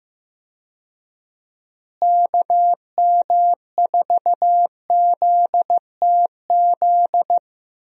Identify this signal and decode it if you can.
Morse code, heard as a keyed tone, about 15 wpm, 700 Hz. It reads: KM4ZTZ
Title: USA Call Signs - 6 Characters - Repeated 3x - 15wpm